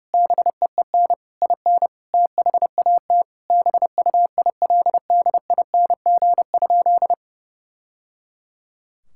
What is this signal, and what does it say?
Morse code, heard as a keyed tone, about 30 wpm, 700 Hz. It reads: BEEN IN THAT BUILDING?
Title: Sentences from Top 1000 Words - Review Part 5 - 30wpm